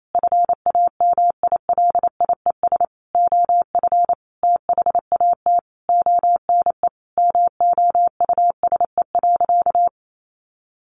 Morse code, heard as a keyed tone, about 28 wpm, 700 Hz. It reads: FAMILIES OF THAT ONE MOUSE.
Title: A Sound Of Thunder 28wpm